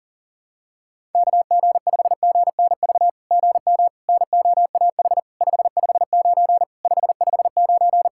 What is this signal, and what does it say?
Morse code, heard as a keyed tone, about 43 wpm, 700 Hz. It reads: KG5GNV GM NOAH 559 559
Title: SOTA QSO Practice 40wpm